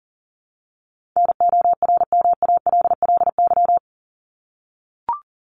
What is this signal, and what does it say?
Morse code, heard as a keyed tone, about 40 wpm, 700 Hz. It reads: NORMALLY
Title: LY Suffix - Top 100 Words with Base Words 40wpm